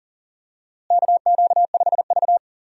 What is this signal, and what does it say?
Morse code, heard as a keyed tone, about 40 wpm, 700 Hz. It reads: KQ5V
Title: USA Call Signs - 4 Characters - Repeated 2x - 40wpm